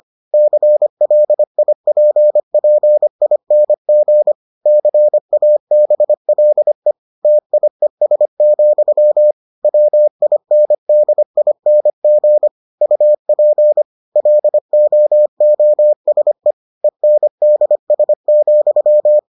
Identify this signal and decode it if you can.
Morse code, heard as a keyed tone, about 25 wpm, 600 Hz. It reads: CLIPPING CABLE TIES, WINDING UP LOOSE ENDS,